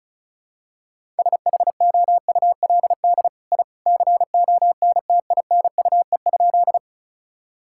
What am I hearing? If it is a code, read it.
Morse code, heard as a keyed tone, about 36 wpm, 700 Hz. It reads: SHOULD I CONTINUE?
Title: Sentences from Top 1000 Words - Review Part 6 - 35wpm